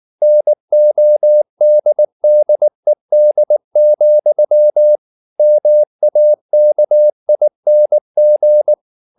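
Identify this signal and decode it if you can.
Morse code, heard as a keyed tone, about 19 wpm, 600 Hz. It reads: NODDED, MAKING
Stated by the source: Morse code at 19 wpm from CW med Gnister, reading Ken Follett's novel Never in English.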